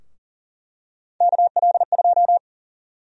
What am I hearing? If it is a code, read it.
Morse code, heard as a keyed tone, about 40 wpm, 700 Hz. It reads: KL2
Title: USA Call Sign Prefixes - 40wpm